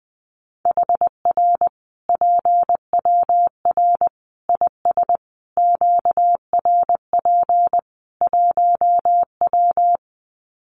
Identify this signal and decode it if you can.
Morse code, heard as a keyed tone, about 20 wpm, 700 Hz. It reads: HR PWR IS QRP 1W
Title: Common QSO Phrases 20wpm